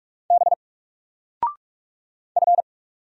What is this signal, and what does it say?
Morse code, heard as a keyed tone, about 44 wpm, 700 Hz. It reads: B F
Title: Single Letter-Number 45wpm